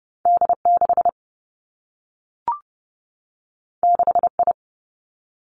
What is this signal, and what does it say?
Morse code, heard as a keyed tone, about 30 wpm, 700 Hz. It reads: D 6 6 I
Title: Two Challenging Letter-Number Combinations 30wpm